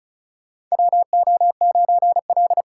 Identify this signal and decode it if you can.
Morse code, heard as a keyed tone, about 34 wpm, 700 Hz. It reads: WO9L
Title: USA Call Signs - 4 Characters - 3x Character Spacing - 35wpm